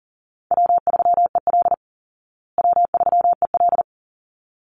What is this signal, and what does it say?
Morse code, heard as a keyed tone, about 40 wpm, 700 Hz. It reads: W3EL W3EL
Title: Easy Call Signs Repeated 2x - 40wpm